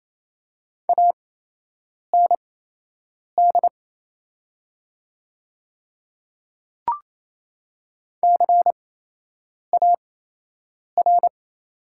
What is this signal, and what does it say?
Morse code, heard as a keyed tone, about 28 wpm, spaced as for 8 wpm, 700 Hz. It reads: AND CAR
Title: 8x Character Spacing - Top 500 Words - 28wpm